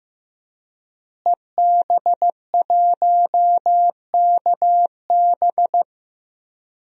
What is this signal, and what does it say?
Morse code, heard as a keyed tone, about 15 wpm, 700 Hz. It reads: EB1KB
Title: DXCC-100 Call Signs - 5 Characters - Repeated 3x - 15wpm